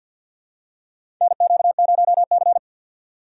Morse code, heard as a keyed tone, about 46 wpm, 700 Hz. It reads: NQ0C
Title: USA Call Signs - 4 Characters - 3x Character Spacing - 50wpm